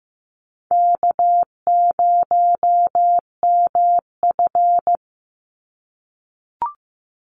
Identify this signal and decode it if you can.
Morse code, heard as a keyed tone, about 15 wpm, 700 Hz. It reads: K0MF
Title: USA Call Signs 15wpm